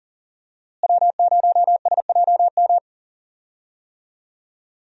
Morse code, heard as a keyed tone, about 40 wpm, 700 Hz. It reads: W0SJM
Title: USA Call Signs - 5 Characters - Repeated 3x - 40wpm